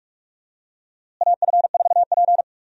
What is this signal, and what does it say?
Morse code, heard as a keyed tone, about 43 wpm, 700 Hz. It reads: AF4P
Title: USA Call Signs - 4 Characters - Repeated 3x - 45wpm